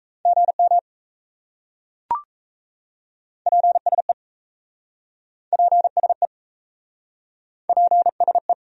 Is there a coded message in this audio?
Yes: Morse code from 0.3 to 8.5 s, heard as a keyed tone, about 38 wpm, 700 Hz. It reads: GM PSE PSE PSE